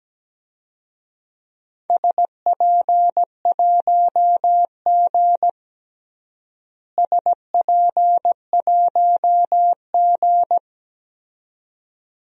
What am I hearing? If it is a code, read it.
Morse code, heard as a keyed tone, about 17 wpm, 700 Hz. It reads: SP1G SP1G
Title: DXCC-100 Call Signs - 4 Characters - Repeated 2x - 17wpm